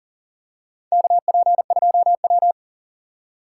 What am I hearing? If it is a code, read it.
Morse code, heard as a keyed tone, about 40 wpm, 700 Hz. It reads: KP2W
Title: USA Call Signs - 4x Character Spacing - 40wpm